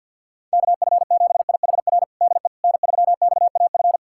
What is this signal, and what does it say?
Morse code, heard as a keyed tone, about 51 wpm, 700 Hz. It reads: KF7IHR DE N3XAF
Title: USA Callsign DE Callsign 50wpm